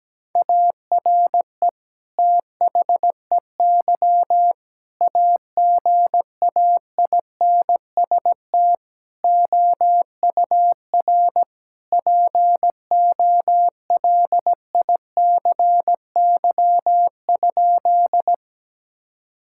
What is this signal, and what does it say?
Morse code, heard as a keyed tone, about 17 wpm, 700 Hz. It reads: ARE THEY AGAINST OUR POLICY?
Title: Sentences from Top 1000 Words - Review Part 5 - 17wpm